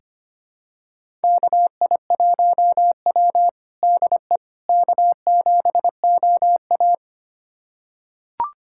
Morse code, heard as a keyed tone, about 25 wpm, 700 Hz. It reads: KI1W DE K7OA
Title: USA Callsign DE Callsign - 4 Characters - 25wpm